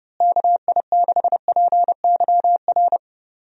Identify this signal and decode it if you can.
Morse code, heard as a keyed tone, about 31 wpm, 700 Hz. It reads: KI6PYR
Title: USA Call Signs - 6 Characters - Repeated 2x - 30wpm